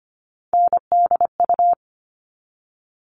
Morse code, heard as a keyed tone, about 25 wpm, 700 Hz. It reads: NDU
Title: All Cut Numbers - 3-Digit 25wpm